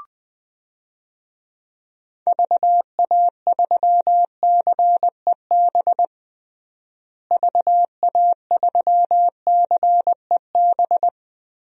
Morse code, heard as a keyed tone, about 21 wpm, 700 Hz. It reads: VA3CEB VA3CEB
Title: DXCC-100 Call Signs - 6 Characters - Repeated 2x - 20wpm